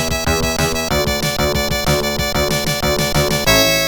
There is no code.